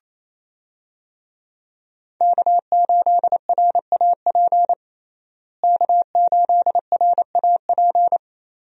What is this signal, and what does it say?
Morse code, heard as a keyed tone, about 28 wpm, 700 Hz. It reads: K8RAP K8RAP
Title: USA Call Signs - 5 Characters - Repeated 3x - 28wpm